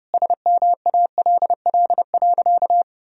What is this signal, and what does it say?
Morse code, heard as a keyed tone, about 30 wpm, 700 Hz. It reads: SMALL.
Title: Sentences from Top 1000 Words - Review Part 1 - 30wpm